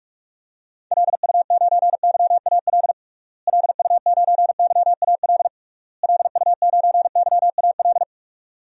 Morse code, heard as a keyed tone, about 46 wpm, 700 Hz. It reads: LU9YAL LU9YAL LU9YAL
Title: DXCC-100 Call Signs - 6 Characters - Repeated 3x - 45wpm